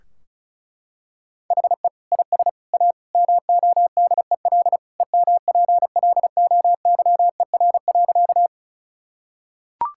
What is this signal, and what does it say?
Morse code, heard as a keyed tone, about 36 wpm, 700 Hz. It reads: HE IS A MODEL EMPLOYER.
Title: Sentences from Top 1600 Words 35wpm